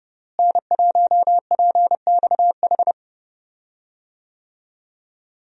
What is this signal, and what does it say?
Morse code, heard as a keyed tone, about 30 wpm, 700 Hz. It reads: N1PXH
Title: USA Call Signs - 5 Characters - Repeated 3x - 30wpm